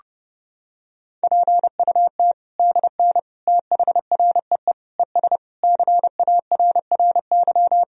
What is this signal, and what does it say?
Morse code, heard as a keyed tone, about 30 wpm, 700 Hz. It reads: PUT DN THREE ES CARRY
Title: Sentences from Top 400 Words and Abbreviations 30wpm